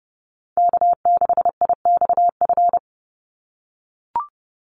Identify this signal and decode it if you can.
Morse code, heard as a keyed tone, about 30 wpm, 700 Hz. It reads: K6IXF
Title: USA Call Signs 30wpm